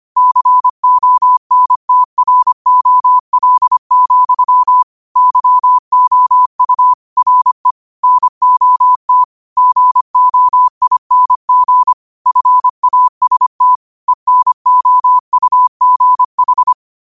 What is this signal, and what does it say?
Morse code, heard as a keyed tone, about 25 wpm, 1000 Hz. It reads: CONTROL, YOU RE NOT GOING FAST ENOUGH
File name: qotdfiles / 25wpm